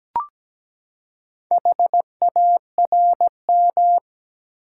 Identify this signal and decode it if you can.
Morse code, heard as a keyed tone, about 17 wpm, 700 Hz. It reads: HARM
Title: Four Letter Words - 17wpm